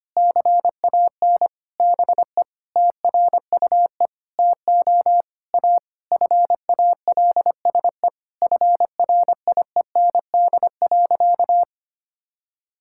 Morse code, heard as a keyed tone, about 25 wpm, 700 Hz. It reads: CAN BE TRUE TO A FALSE FRIEND.
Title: Sentences from Top 1600 Words 25wpm